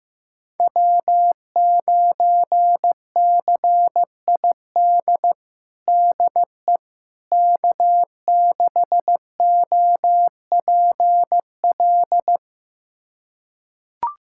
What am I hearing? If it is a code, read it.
Morse code, heard as a keyed tone, about 15 wpm, 700 Hz. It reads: W9CID DE K6OPL
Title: USA Callsign DE Callsign - 5 Characters - 15wpm